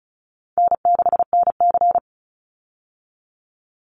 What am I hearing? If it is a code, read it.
Morse code, heard as a keyed tone, about 34 wpm, 700 Hz. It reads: N6NC
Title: Easy Call Signs Repeated 2x - 35wpm